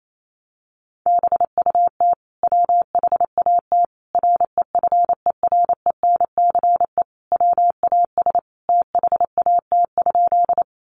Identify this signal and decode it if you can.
Morse code, heard as a keyed tone, about 28 wpm, 700 Hz. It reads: BUT WHAT REFERENCE WAS THAT?
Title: Sentences from Top 1200 Words 28wpm